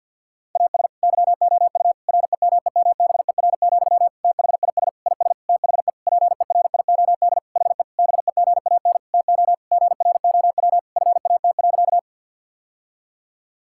Morse code, heard as a keyed tone, about 50 wpm, 700 Hz. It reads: AS YOU REMEMBER, THIS IS THE PERIOD HE BEGAN TO GROW FAT.
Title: Sentences from Top 1000 Words - Review Part 5 - 50wpm